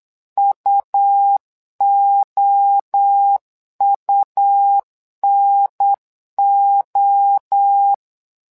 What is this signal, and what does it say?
Morse code, heard as a keyed tone, about 8 wpm, 800 Hz. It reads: UOUNO
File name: LEZIONI / 3 MWRO / 3 MWRO 8wpm 2606